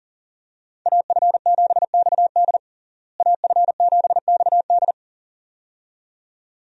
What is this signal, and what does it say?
Morse code, heard as a keyed tone, about 40 wpm, 700 Hz. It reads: AF7XD AF7XD
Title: USA Call Signs - 5 Characters - Repeated 2x - 40wpm